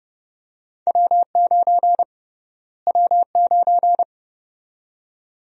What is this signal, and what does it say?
Morse code, heard as a keyed tone, about 31 wpm, 700 Hz. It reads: W9 W9
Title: USA Call Sign Prefixes - Repeated 2x - 30wpm